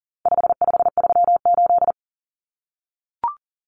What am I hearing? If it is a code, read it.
Morse code, heard as a keyed tone, about 40 wpm, 700 Hz. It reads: 5538
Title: Common Cut Numbers - 4-Digit 40wpm